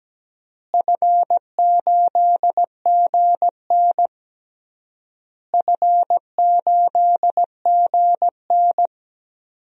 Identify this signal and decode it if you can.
Morse code, heard as a keyed tone, about 17 wpm, 700 Hz. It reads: F8GN F8GN
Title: DXCC-100 Call Signs - 4 Characters - Repeated 3x - 17wpm